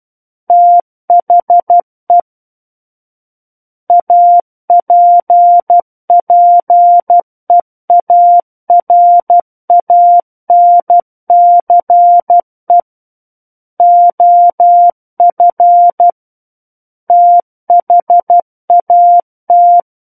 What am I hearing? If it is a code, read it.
Morse code, heard as a keyed tone, about 12 wpm, 700 Hz. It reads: THE APPEARANCE OF THAT
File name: Book One / 03-Chapter 3 - 12 WPM